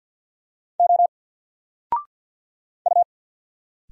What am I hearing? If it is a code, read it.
Morse code, heard as a keyed tone, about 50 wpm, 700 Hz. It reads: O U
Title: Single Character (Letter:Number:Punctuation:Prosign) 50wpm